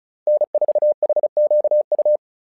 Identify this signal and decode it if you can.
Morse code, heard as a keyed tone, about 34 wpm, 600 Hz. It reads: N4HQU
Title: SOTA QSO Practice 35wpm